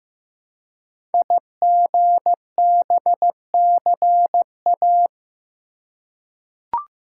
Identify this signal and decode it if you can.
Morse code, heard as a keyed tone, about 15 wpm, 700 Hz. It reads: IGBCA
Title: Code Groups - 5 Letters - 15wpm